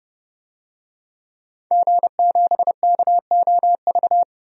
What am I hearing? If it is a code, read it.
Morse code, heard as a keyed tone, about 30 wpm, 700 Hz. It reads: G7KOV
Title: DXCC-100 Call Signs - 5 Characters - 30wpm